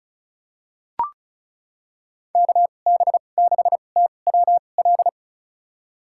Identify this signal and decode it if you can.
Morse code, heard as a keyed tone, about 36 wpm, spaced as for 25 wpm, 700 Hz. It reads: KB6TWL
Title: USA Call Signs - 2x Character Spacing - 35wpm